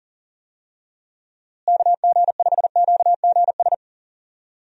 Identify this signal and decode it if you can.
Morse code, heard as a keyed tone, about 37 wpm, 700 Hz. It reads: KG5QGS
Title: USA Call Signs - 6 Characters - Repeated 3x - 40wpm